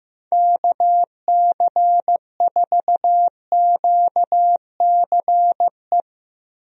Morse code, heard as a keyed tone, about 15 wpm, 700 Hz. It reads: KC4QCE
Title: USA Call Signs - 6 Characters - 15wpm